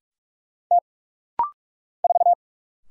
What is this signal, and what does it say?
Morse code, heard as a keyed tone, about 46 wpm, 700 Hz. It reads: T 4
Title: Single Letter-Number - Rapid-Fire 45wpm